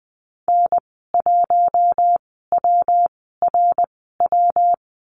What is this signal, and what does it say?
Morse code, heard as a keyed tone, about 20 wpm, spaced as for 14 wpm, 700 Hz. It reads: N1WRW
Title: 147 Lesson 24g - Letter y - Call Signs 20wpm